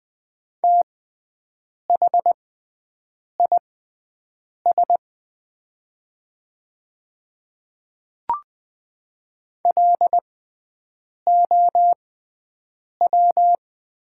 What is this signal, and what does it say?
Morse code, heard as a keyed tone, about 20 wpm, spaced as for 7 wpm, 700 Hz. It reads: THIS LOW
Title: 6x Character Spacing - Top 200 Words - 20wpm